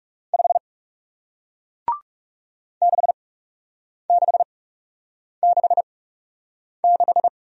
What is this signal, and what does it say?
Morse code, heard as a keyed tone, about 40 wpm, spaced as for 8 wpm, 700 Hz. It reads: 5 6666